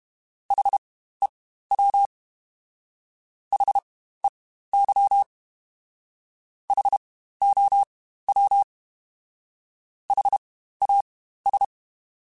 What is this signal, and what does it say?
Morse code, heard as a keyed tone, about 32 wpm, spaced as for 15 wpm, 800 Hz. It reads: HEW HEY HOW HAS
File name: Spaced Alphas / Spaced Alphas mp3s / Hs sp @30